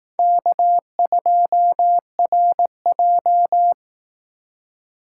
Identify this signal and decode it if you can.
Morse code, heard as a keyed tone, about 18 wpm, 700 Hz. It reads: K2RJ
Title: Speed Racing - USA 1x2 Call Signs 15wpm